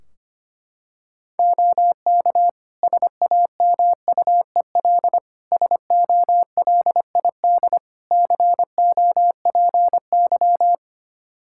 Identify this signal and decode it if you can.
Morse code, heard as a keyed tone, about 25 wpm, 700 Hz. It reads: OK SAMUEL SOLID COPY
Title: K1USN SST QSO Practice 25wpm